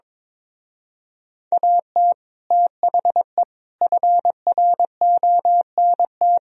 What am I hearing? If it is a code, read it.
Morse code, heard as a keyed tone, about 23 wpm, 700 Hz. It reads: AT THE FRONT
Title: Speed-Racing - Sets of 5 Words 22wpm